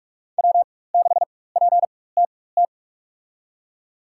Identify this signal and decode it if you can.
Morse code, heard as a keyed tone, about 46 wpm, spaced as for 21 wpm, 700 Hz. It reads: W6PTT
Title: USA Call Signs - 5 Characters - 4x Character Spacing - 45wpm